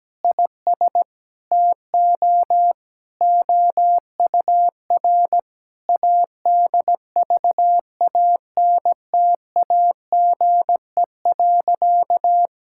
Morse code, heard as a keyed tone, about 17 wpm, 700 Hz. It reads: IS TO OUR ADVANTAGE.
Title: Sentences from Top 1000 Words - Review Part 6 - 17wpm